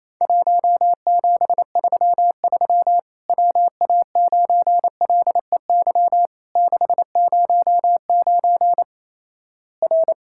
Morse code, heard as a keyed tone, about 28 wpm, 700 Hz. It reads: 1733 WA9LEY 609 R
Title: Worked All Europe DX Contest - QTC Exchange - 28wpm